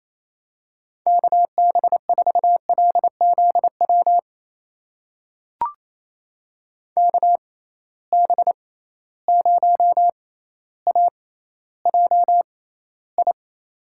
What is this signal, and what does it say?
Morse code, heard as a keyed tone, about 28 wpm, spaced as for 14 wpm, 700 Hz. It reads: KB4LZW KB0AJI
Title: USA Call Signs - 6 Characters - 6x Character Spacing - 28wpm